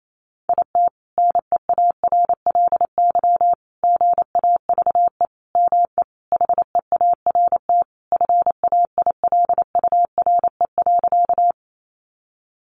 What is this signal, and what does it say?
Morse code, heard as a keyed tone, about 28 wpm, 700 Hz. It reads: IT NEARLY GAVE ME HEART FAILURE.
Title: Sentences from Top 1300 Words 28wpm